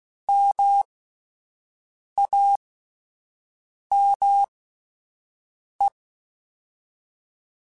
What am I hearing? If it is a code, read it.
Morse code, heard as a keyed tone, about 16 wpm, spaced as for 5 wpm, 800 Hz. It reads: MAME